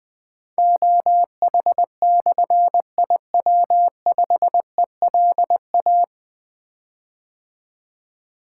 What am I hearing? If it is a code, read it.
Morse code, heard as a keyed tone, about 20 wpm, 700 Hz. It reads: OH/IW5ELA
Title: Difficult Call Signs 20wpm